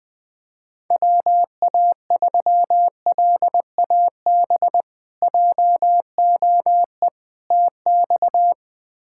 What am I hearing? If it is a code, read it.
Morse code, heard as a keyed tone, about 21 wpm, 700 Hz. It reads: WA3LAB JOE TX
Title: K1USN SST QSO Practice - 20wpm